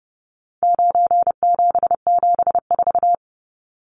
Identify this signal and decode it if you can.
Morse code, heard as a keyed tone, about 30 wpm, 700 Hz. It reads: 9774
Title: Numbers - 4-Digit 30wpm